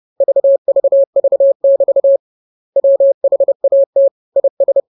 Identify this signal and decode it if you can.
Morse code, heard as a keyed tone, about 30 wpm, 550 Hz. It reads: VVV<BT> WHAT IS
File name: jokes / joke 6 30 0000